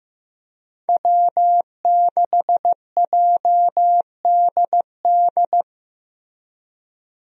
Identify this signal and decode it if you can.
Morse code, heard as a keyed tone, about 15 wpm, 700 Hz. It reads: W6JDD
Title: USA Call Signs - 5 Characters - Repeated 2x - 15wpm